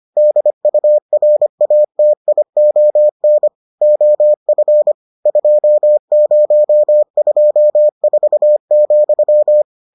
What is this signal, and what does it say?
Morse code, heard as a keyed tone, about 25 wpm, 600 Hz. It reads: DURATION OF 2024,